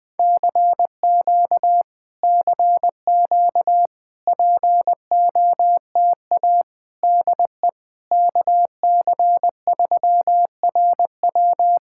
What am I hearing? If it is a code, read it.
Morse code, heard as a keyed tone, about 21 wpm, 700 Hz. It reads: CQ CQ POTA DE KC3RW
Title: POTA QSO Practice - 20wpm